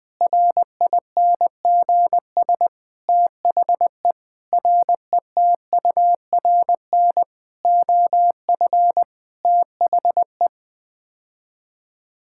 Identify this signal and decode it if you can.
Morse code, heard as a keyed tone, about 19 wpm, 700 Hz. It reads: RINGS THE RETURN OF THE
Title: Sets of 9 Words - 20wpm